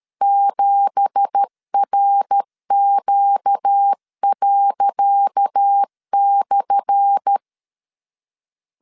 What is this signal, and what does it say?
Morse code, heard as a keyed tone, about 13 wpm, 800 Hz. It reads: 7RQ./
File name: LEZIONI / 10 SIGNS / 10C DOT 12wpm 2451